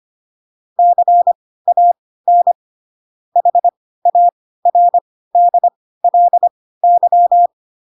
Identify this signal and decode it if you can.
Morse code, heard as a keyed tone, about 25 wpm, spaced as for 16 wpm, 700 Hz. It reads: CAN HARDLY
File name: Pinocchio / Pinocchio ch1 10 25 ik1hge